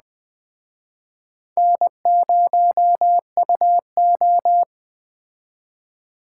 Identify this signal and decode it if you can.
Morse code, heard as a keyed tone, about 20 wpm, 700 Hz. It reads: N0UO
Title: Speed Racing - USA 1x2 Call Signs 17wpm